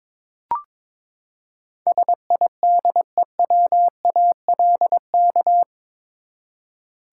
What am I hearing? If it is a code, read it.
Morse code, heard as a keyed tone, about 22 wpm, 700 Hz. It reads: SIDEWALK
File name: practice / 22wpm / Eight Letter Words - 22wpm